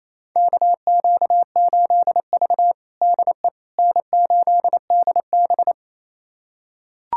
Morse code, heard as a keyed tone, about 28 wpm, 700 Hz. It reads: KQ8V DE N8DB
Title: USA Callsign DE Callsign - 4 Characters - 28wpm